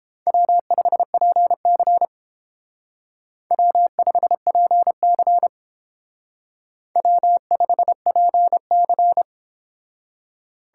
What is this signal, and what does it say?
Morse code, heard as a keyed tone, about 30 wpm, 700 Hz. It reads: W5PC W5PC W5PC